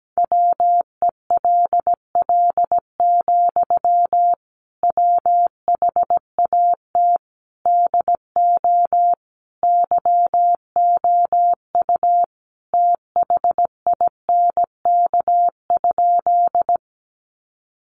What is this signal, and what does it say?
Morse code, heard as a keyed tone, about 17 wpm, 700 Hz. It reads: WELL, WHAT DO YOU THINK?
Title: Sentences from Top 200 Words 17wpm